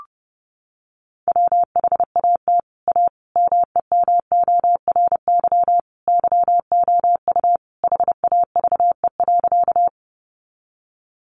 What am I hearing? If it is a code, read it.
Morse code, heard as a keyed tone, about 30 wpm, 700 Hz. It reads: WHAT A MEMORY YOU HAVE.
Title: Sentences from Top 1300 Words 30wpm